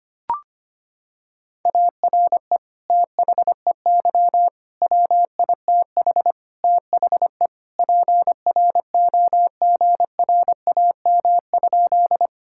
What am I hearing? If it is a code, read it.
Morse code, heard as a keyed tone, about 25 wpm, 700 Hz. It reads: ARE THEY WITH THE PROGRAM?
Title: Sentences from Top 1000 Words - Review Part 5 - 25wpm